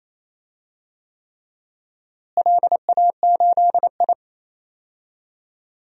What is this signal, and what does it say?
Morse code, heard as a keyed tone, about 28 wpm, 700 Hz. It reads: LA8I
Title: DXCC-100 Call Signs - 4 Characters - Repeated 3x - 28wpm